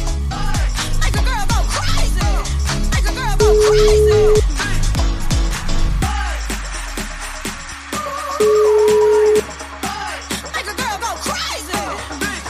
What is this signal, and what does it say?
Ringback tone (Europe, 425 Hz), with noise.